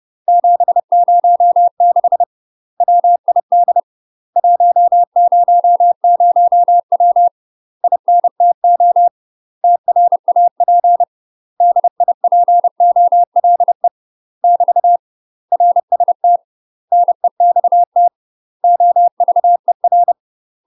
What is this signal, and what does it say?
Morse code, heard as a keyed tone, about 30 wpm, 700 Hz. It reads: 706 WID 100W INTO TRAP DIPOLE <BT> RST NEXT OVER
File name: UK MASTER QSOs TRAINING / 30WPM / UK MASTER QSOs 0024